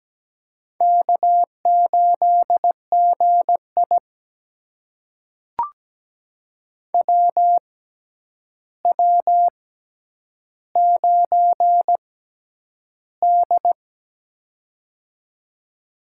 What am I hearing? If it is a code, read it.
Morse code, heard as a keyed tone, about 17 wpm, spaced as for 9 wpm, 700 Hz. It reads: K8GI WW9D
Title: USA Call Signs - 4 Characters - 6x Character Spacing - 17wpm